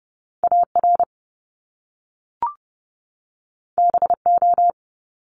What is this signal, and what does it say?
Morse code, heard as a keyed tone, about 30 wpm, 700 Hz. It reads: AR BO
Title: Top Two Letter Combinations in Words 30wpm